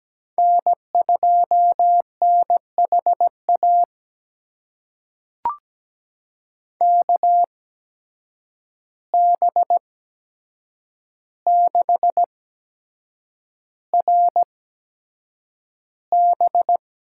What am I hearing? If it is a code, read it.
Morse code, heard as a keyed tone, about 17 wpm, spaced as for 7 wpm, 700 Hz. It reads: N2NHA KB6RB
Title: USA Call Signs - 5 Characters - 8x Character Spacing - 17wpm